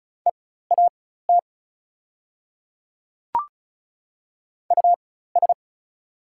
Morse code, heard as a keyed tone, about 36 wpm, spaced as for 16 wpm, 700 Hz. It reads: EAT US
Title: 4x Character Spacing - Top 300 Words - 35wpm